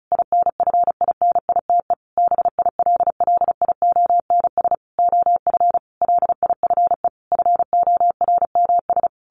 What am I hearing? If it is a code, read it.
Morse code, heard as a keyed tone, about 34 wpm, 700 Hz. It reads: INFINITE BILLIONS OF LIFE FORMS
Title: A Sound Of Thunder 35wpm